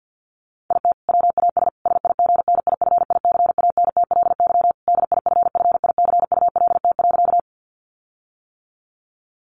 Morse code, heard as a keyed tone, about 50 wpm, 700 Hz. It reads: IT WAS SIGNIFICANTLY DIFFICULT.
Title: Sentences from Top 1500 Words 50wpm